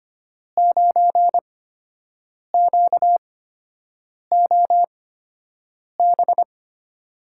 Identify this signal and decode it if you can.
Morse code, heard as a keyed tone, about 25 wpm, spaced as for 7 wpm, 700 Hz. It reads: 9QOB
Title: USA Call Signs - 5 Characters - 8x Character Spacing - 25wpm